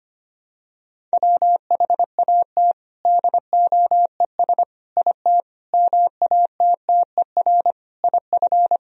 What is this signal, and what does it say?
Morse code, heard as a keyed tone, about 25 wpm, 700 Hz. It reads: WHAT DOES IT MATTER IF
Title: Sentences from Top 1000 Words - Review Part 1 - 25wpm